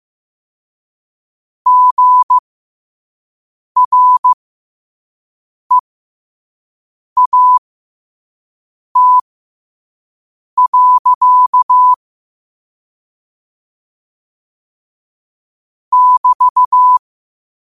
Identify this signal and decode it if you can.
Morse code, heard as a keyed tone, about 15 wpm, spaced as for 5 wpm, 1000 Hz. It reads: GREAT. <BT>
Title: 5 WPM morse code quotes for Fri, 01 Mar 2024